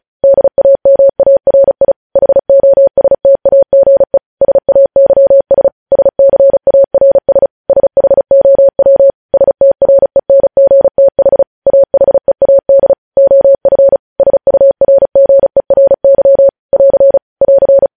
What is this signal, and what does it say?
Morse code, heard as a keyed tone, about 36 wpm, 550 Hz. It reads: DAMARI HOSTAGE SAYS SCARS SHOW STRENGTH AHEAD OF SURGERY <AR> <AR>
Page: News Headlines in Morse Code at 35 WPM